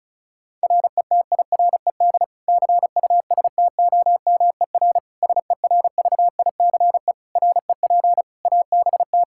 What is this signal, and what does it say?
Morse code, heard as a keyed tone, about 36 wpm, 700 Hz. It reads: RETIRED CUSTOMER SERVICE REP ABT
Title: K1USN SST QSO Practice 35wpm